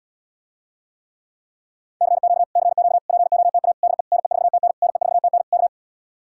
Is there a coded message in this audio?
Yes: Morse code from 2.0 to 5.7 s, heard as a keyed tone, about 77 wpm, 700 Hz. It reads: CQ CQ POTA DE NE4TN NE4TN K